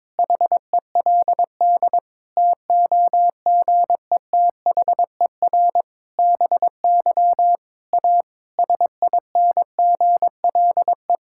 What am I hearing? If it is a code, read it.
Morse code, heard as a keyed tone, about 22 wpm, 700 Hz. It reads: HELD TOGETHER BY A SINGLE
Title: Sentences from Top 1600 Words 22wpm